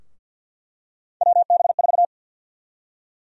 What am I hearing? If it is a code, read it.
Morse code, heard as a keyed tone, about 50 wpm, 700 Hz. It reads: WB4
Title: USA Call Sign Prefixes - Repeated 2x - 50wpm